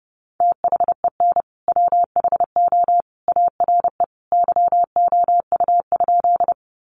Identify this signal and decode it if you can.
Morse code, heard as a keyed tone, about 30 wpm, 700 Hz. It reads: THEN WHO ARE YOU?
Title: Sentences from Top 100 Words 30wpm